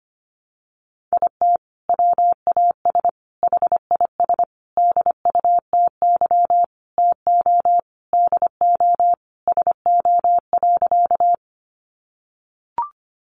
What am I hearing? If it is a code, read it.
Morse code, heard as a keyed tone, about 25 wpm, 700 Hz. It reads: IT WAS HIS DUTY TO DO SO.
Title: Sentences from Top 1500 Words 25wpm